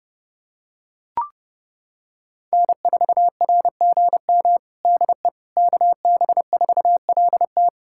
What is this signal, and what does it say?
Morse code, heard as a keyed tone, about 32 wpm, 700 Hz. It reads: N4RGM DE KB4LT
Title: USA Callsign DE Callsign - 5 Characters - 30wpm